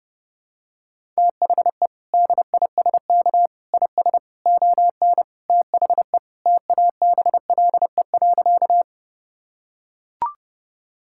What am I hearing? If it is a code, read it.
Morse code, heard as a keyed tone, about 30 wpm, 700 Hz. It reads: THE DISK IS ON THE TABLE.
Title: Sentences from Top 1000 Words - Review Part 6 - 30wpm